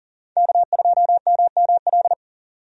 Morse code, heard as a keyed tone, about 40 wpm, 700 Hz. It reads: K2MML
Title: USA Call Signs - 5 Characters - Repeated 3x - 40wpm